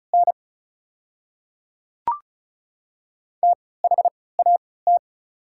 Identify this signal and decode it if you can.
Morse code, heard as a keyed tone, about 36 wpm, spaced as for 20 wpm, 690 Hz. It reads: N THAT
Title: 3x Character Spacing - Top 100 Words - 35wpm